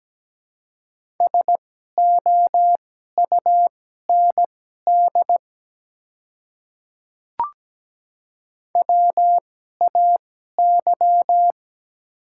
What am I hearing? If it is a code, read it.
Morse code, heard as a keyed tone, about 17 wpm, spaced as for 12 wpm, 700 Hz. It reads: SOUND WAY
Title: 2x Character Spacing - Top 100 Words - 17wpm